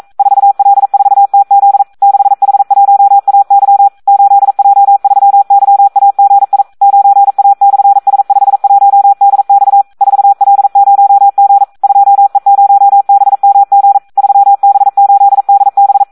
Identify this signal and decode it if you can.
Morse code, heard as a keyed tone, about 42 wpm, 800 Hz. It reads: 4L4TZ 6H1AY 8J3YAGI 9A/S51DX 4L0G 2E0BMG 3B8DB
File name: Audio practica CW / SD 1 42